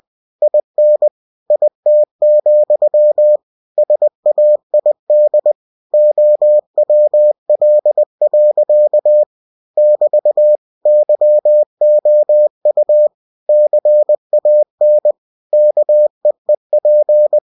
Morse code, heard as a keyed tone, about 20 wpm, 600 Hz. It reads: IN IT, SAID OWL. <BT> YOU CAN KEEP
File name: cw-books / winnie-the-pooh / winnie-20wpm-page-0051